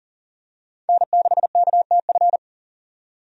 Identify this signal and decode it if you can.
Morse code, heard as a keyed tone, about 40 wpm, 700 Hz. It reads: N6KTF
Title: USA Call Signs - 5 Characters - 40wpm